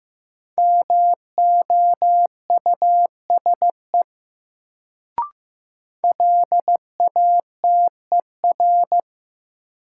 Morse code, heard as a keyed tone, about 15 wpm, 700 Hz. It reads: MOUSE LATER
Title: Sentences from Top 700 Words - New Word Drill - 15wpm